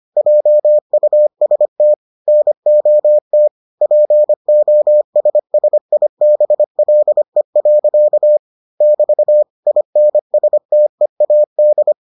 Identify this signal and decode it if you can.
Morse code, heard as a keyed tone, about 25 wpm, 600 Hz. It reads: JUST NOT POSSIBLE. <BT> INSTEAD